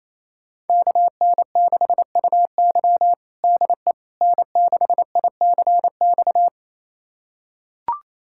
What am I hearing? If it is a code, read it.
Morse code, heard as a keyed tone, about 28 wpm, 700 Hz. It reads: KN6UY DE N6ICX
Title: USA Callsign DE Callsign - 5 Characters - 28wpm